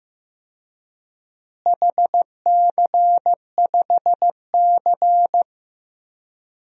Morse code, heard as a keyed tone, about 15 wpm, 700 Hz. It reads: HC5C
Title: DXCC-100 Call Signs - 4 Characters - Repeated 3x - 15wpm